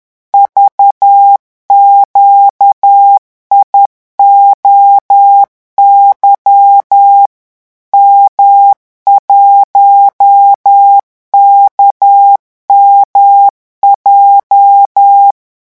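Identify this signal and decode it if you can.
Morse code, heard as a keyed tone, about 10 wpm, 800 Hz. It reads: VQIOY M1KMJ
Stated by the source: Morse revision exercise for French radio amateurs at 10 words per minute.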